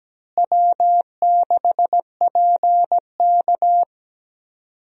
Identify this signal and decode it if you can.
Morse code, heard as a keyed tone, about 17 wpm, 700 Hz. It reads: W6PK
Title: USA Call Signs - 4 Characters - Repeated 2x - 17wpm